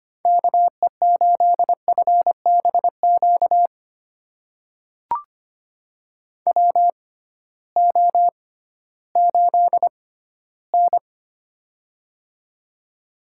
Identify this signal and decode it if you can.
Morse code, heard as a keyed tone, about 25 wpm, spaced as for 14 wpm, 700 Hz. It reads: KE8FBQ WO8N
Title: USA Call Signs - 6x Character Spacing - 25wpm